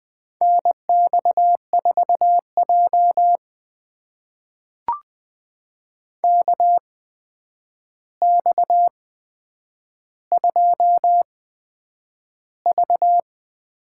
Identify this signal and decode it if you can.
Morse code, heard as a keyed tone, about 20 wpm, spaced as for 9 wpm, 700 Hz. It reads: NX4J KX2V
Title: USA Call Signs - 4 Characters - 8x Character Spacing - 20wpm